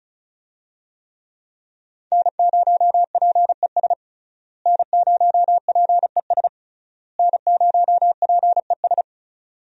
Morse code, heard as a keyed tone, about 34 wpm, 700 Hz. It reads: N0PES N0PES N0PES
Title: USA Call Signs - 5 Characters - Repeated 3x - 35wpm